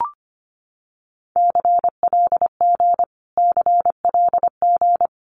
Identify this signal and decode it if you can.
Morse code, heard as a keyed tone, about 25 wpm, 700 Hz. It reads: CLG CLG
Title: Abbreviations Repeated 3x - 25wpm